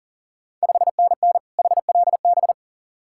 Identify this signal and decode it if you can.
Morse code, heard as a keyed tone, about 40 wpm, 690 Hz. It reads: 5NN HLB
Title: Ontario QSO Party Exchange 40wpm